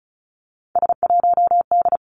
Morse code, heard as a keyed tone, about 36 wpm, 700 Hz. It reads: S1D
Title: Three Challenging Letter-Number Combinations 35wpm